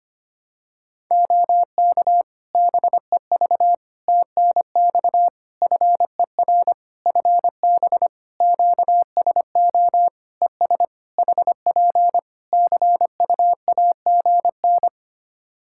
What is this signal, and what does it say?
Morse code, heard as a keyed tone, about 25 wpm, 700 Hz. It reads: OK BEV TNX FER FB QSO ES HP CUAGN
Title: Practice QSO - Exchange 4 - Ending - 25wpm